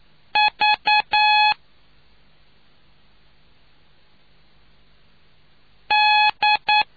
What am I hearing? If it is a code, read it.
Morse code, heard as a keyed tone, about 9 wpm, 840 Hz. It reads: V D